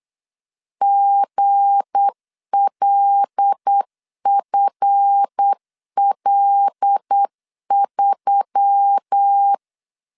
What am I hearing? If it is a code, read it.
Morse code, heard as a keyed tone, about 8 wpm, 800 Hz. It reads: GLFL3
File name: LEZIONI / 7 GF9Z / 7 GF 8wpm 3108